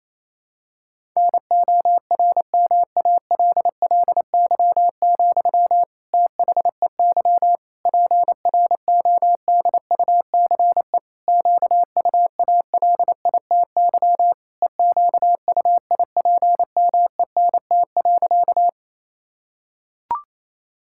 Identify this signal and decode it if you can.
Morse code, heard as a keyed tone, about 28 wpm, 700 Hz. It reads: NORMALLY, THEY PRODUCE QUALITY EQUIPMENT.
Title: Sentences from Top 1000 Words - Review Part 6 - 28wpm